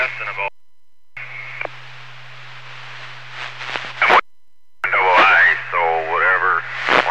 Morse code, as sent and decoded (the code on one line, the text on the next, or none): none